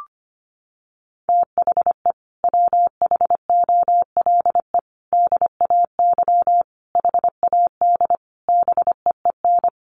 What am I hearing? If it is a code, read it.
Morse code, heard as a keyed tone, about 25 wpm, 700 Hz. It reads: THE WHOLE DAY HAD BEEN
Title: Sentences from Top 500 Words 25wpm